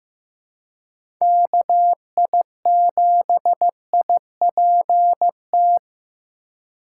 Morse code, heard as a keyed tone, about 16 wpm, 700 Hz. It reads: KI7IPT
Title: USA Call Signs - 3x Character Spacing - 15wpm